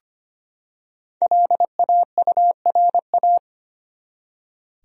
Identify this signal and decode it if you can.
Morse code, heard as a keyed tone, about 25 wpm, 700 Hz. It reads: LAURA
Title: USA Common Names - Max 5 Letters - 25wpm